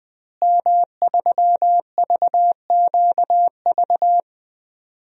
Morse code, heard as a keyed tone, about 20 wpm, 700 Hz. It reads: M3VQV
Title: DXCC-100 Call Signs - 5 Characters - Repeated 3x - 20wpm